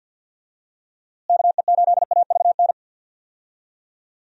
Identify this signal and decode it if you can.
Morse code, heard as a keyed tone, about 49 wpm, 700 Hz. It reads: KE8AVN
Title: USA Call Signs - 6 Characters - Repeated 3x - 50wpm